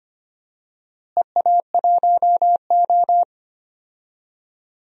Morse code, heard as a keyed tone, about 25 wpm, 700 Hz. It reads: EA1O
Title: DXCC-100 Call Signs - 4 Characters - Repeated 2x - 25wpm